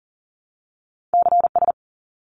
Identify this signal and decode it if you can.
Morse code, heard as a keyed tone, about 40 wpm, 700 Hz. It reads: CS
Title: Abbreviations 40wpm